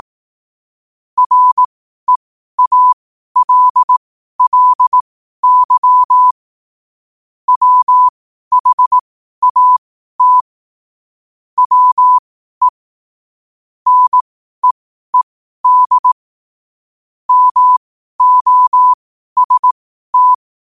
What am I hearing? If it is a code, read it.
Morse code, heard as a keyed tone, about 18 wpm, spaced as for 12 wpm, 1000 Hz. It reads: REALLY WHAT WE NEED MOST
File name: qotdfiles / 12wpm